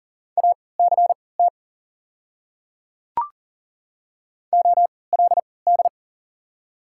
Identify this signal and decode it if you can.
Morse code, heard as a keyed tone, about 39 wpm, spaced as for 22 wpm, 690 Hz. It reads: ACT OLD
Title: 3x Character Spacing - Top 200 Words - 40wpm